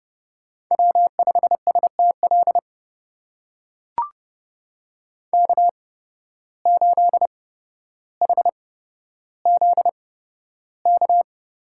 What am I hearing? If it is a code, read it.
Morse code, heard as a keyed tone, about 29 wpm, spaced as for 13 wpm, 700 Hz. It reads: W5STL K8HZK
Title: USA Call Signs - 5 Characters - 8x Character Spacing - 30wpm